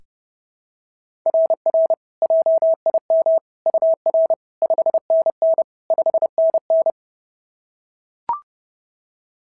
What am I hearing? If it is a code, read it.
Morse code, heard as a keyed tone, about 30 wpm, 660 Hz. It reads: RR JIM UR 5NN 5NN
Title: SOTA QSO Practice 30wpm